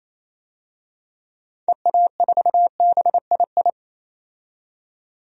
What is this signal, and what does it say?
Morse code, heard as a keyed tone, about 28 wpm, 700 Hz. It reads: EA4BII
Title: DXCC-100 Call Signs - 6 Characters - 28wpm